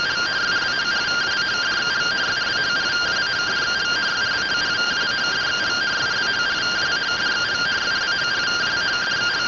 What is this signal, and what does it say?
Two-tone teleprinter signal, about 170 Hz shift.